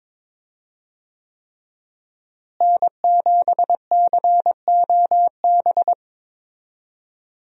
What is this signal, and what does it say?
Morse code, heard as a keyed tone, about 22 wpm, 700 Hz. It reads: N7COB
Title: USA Call Signs - 5 Characters - Repeated 3x - 22wpm